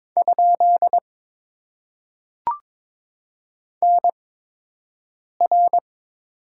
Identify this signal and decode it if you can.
Morse code, heard as a keyed tone, about 23 wpm, spaced as for 6 wpm, 700 Hz. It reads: ? NR
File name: Top 90 QSO Elements - 8x Character Spacing - 22wpm